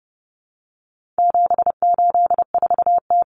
Morse code, heard as a keyed tone, about 30 wpm, 700 Hz. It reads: 784T
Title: Common Cut Numbers - 4-Digit 30wpm